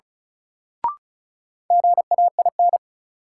Morse code, heard as a keyed tone, about 34 wpm, 700 Hz. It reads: GAIN
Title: Sentences from Top 1000 Words - New Word Drill - 35wpm